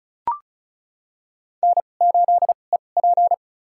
Morse code, heard as a keyed tone, about 34 wpm, spaced as for 25 wpm, 700 Hz. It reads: N8EP
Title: USA Call Signs - 2x Character Spacing - 35wpm